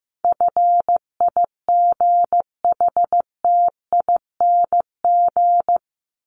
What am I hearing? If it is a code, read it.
Morse code, heard as a keyed tone, about 14 wpm, 700 Hz. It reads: FIGHTING
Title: ING Suffix - Top 100 Words with Base Words 15wpm